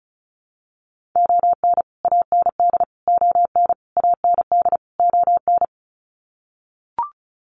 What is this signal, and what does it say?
Morse code, heard as a keyed tone, about 34 wpm, 700 Hz. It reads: ON AND ON AND ON
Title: Sets of 5 Words Encore2 35wpm